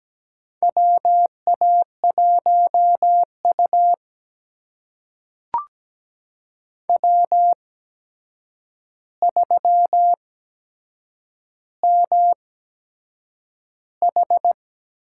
Morse code, heard as a keyed tone, about 17 wpm, spaced as for 7 wpm, 700 Hz. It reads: WA1U W3MH
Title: USA Call Signs - 8x Character Spacing - 17wpm